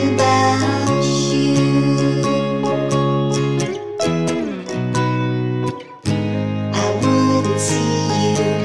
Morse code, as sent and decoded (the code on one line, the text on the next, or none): none